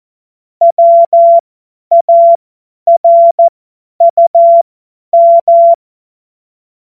Morse code, heard as a keyed tone, about 14 wpm, spaced as for 10 wpm, 680 Hz. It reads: WARUM
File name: Morsekurs / Witze / Witz 13. September 2022 50 F70